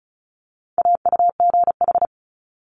Morse code, heard as a keyed tone, about 36 wpm, 700 Hz. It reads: AUGH
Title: Multi-letter Phonemes Quick Fire 35wpm